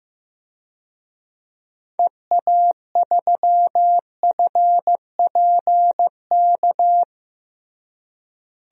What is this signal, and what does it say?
Morse code, heard as a keyed tone, about 15 wpm, 700 Hz. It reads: EA3FPK
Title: DXCC-100 Call Signs - 6 Characters - Repeated 2x - 15wpm